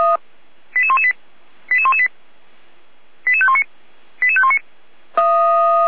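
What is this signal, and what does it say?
Five-tone selective call (ZVEI), heard as quick runs of five tones, tones of about 70 ms. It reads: 88118 88118 88519 88519
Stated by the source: ZVEI five-tone call 88517 received by SDR, logged on the audio server of feuerwehr-orlamuende.org.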